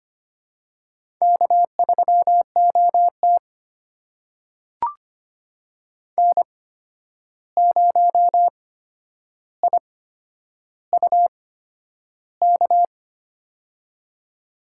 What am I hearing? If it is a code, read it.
Morse code, heard as a keyed tone, about 25 wpm, spaced as for 10 wpm, 700 Hz. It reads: K3OT N0IUK
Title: USA Call Signs - 8x Character Spacing - 25wpm